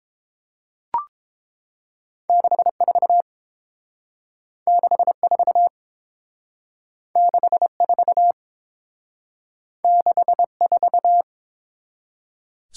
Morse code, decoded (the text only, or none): 64 64 64 64